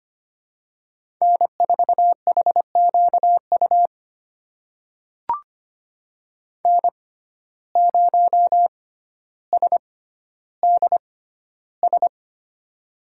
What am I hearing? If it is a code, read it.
Morse code, heard as a keyed tone, about 25 wpm, spaced as for 13 wpm, 700 Hz. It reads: N4HQU N0SDS
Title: USA Call Signs - 5 Characters - 6x Character Spacing - 25wpm